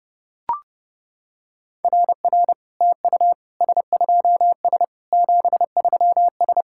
Morse code, heard as a keyed tone, about 30 wpm, 700 Hz. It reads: RR TU S2S 73S
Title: SOTA QSO Practice 30wpm